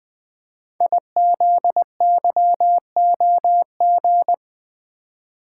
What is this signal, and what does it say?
Morse code, heard as a keyed tone, about 20 wpm, 700 Hz. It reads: IZYOG
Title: Code Groups - 5 Letters - 20wpm